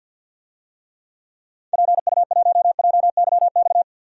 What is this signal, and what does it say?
Morse code, heard as a keyed tone, about 51 wpm, 700 Hz. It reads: PU1JYX
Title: DXCC-100 Call Signs - 6 Characters - 50wpm